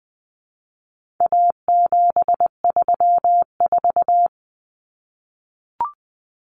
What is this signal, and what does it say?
Morse code, heard as a keyed tone, about 20 wpm, 700 Hz. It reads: A734
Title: Common Cut Numbers - 4-Digit 20wpm